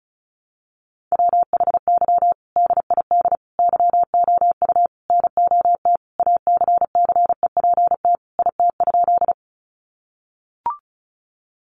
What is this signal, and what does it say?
Morse code, heard as a keyed tone, about 34 wpm, 700 Hz. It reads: WHY DID YOU NOT ACCEPT IT?
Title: Sentences from Top 1300 Words 35wpm